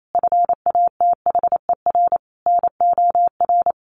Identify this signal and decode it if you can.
Morse code, heard as a keyed tone, about 28 wpm, 700 Hz. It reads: FATHER NOR
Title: Sentences from Top 1500 Words 28wpm